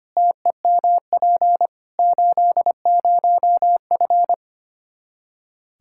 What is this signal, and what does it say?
Morse code, heard as a keyed tone, about 25 wpm, 700 Hz. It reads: TEMP 80F
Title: Practice QSO - Exchange 2 - Info - 25wpm